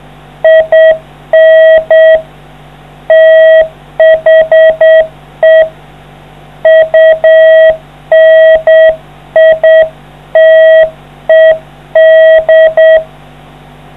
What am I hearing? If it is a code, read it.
Morse code, heard as a keyed tone, about 8 wpm, 640 Hz. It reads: IN THE UNITED